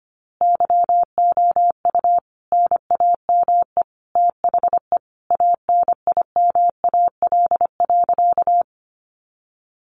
Morse code, heard as a keyed tone, about 25 wpm, 700 Hz. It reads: YOU NAME THE ANIMAL.
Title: A Sound Of Thunder 25wpm